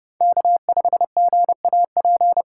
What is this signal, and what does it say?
Morse code, heard as a keyed tone, about 31 wpm, 700 Hz. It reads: K5GAP
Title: USA Call Signs - 5 Characters - 4x Character Spacing - 30wpm